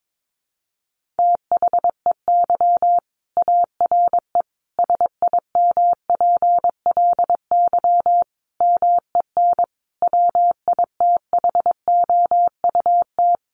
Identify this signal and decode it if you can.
Morse code, heard as a keyed tone, about 22 wpm, 700 Hz. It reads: THEY ARE SIMPLY MEN WITHOUT
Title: Sentences from Top 1500 Words 22wpm